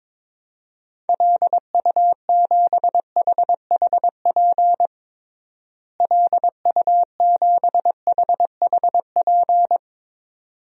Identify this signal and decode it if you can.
Morse code, heard as a keyed tone, about 22 wpm, 700 Hz. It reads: LU7HHP LU7HHP
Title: DXCC-100 Call Signs - 6 Characters - Repeated 2x - 22wpm